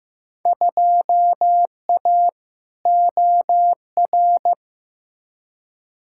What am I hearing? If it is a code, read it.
Morse code, heard as a keyed tone, about 15 wpm, 700 Hz. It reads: 2A OR
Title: Field Day Exchanges - 15wpm